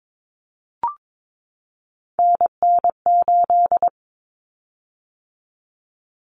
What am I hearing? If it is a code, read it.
Morse code, heard as a keyed tone, about 22 wpm, 700 Hz. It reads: NN8
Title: Common Cut Numbers - 3-Digit 22wpm